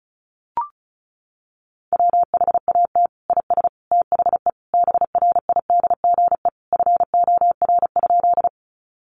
Morse code, heard as a keyed tone, about 34 wpm, 700 Hz. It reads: WHAT IS THE BRIDGE FOR?
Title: Sentences from Top 1200 Words 35wpm